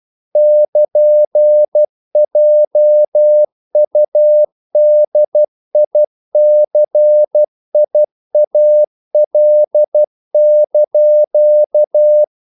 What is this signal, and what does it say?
Morse code, heard as a keyed tone, about 12 wpm, 600 Hz. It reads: <KN>JUDICIAL)
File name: CW / Practice / NationalCapitals12